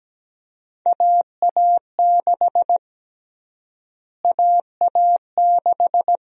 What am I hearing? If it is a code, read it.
Morse code, heard as a keyed tone, about 17 wpm, 700 Hz. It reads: AA6 AA6
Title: USA Call Sign Prefixes - Repeated 2x - 17wpm